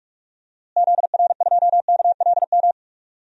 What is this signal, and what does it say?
Morse code, heard as a keyed tone, about 46 wpm, 700 Hz. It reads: ZR2KLM
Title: DXCC-100 Call Signs - 6 Characters - Repeated 2x - 45wpm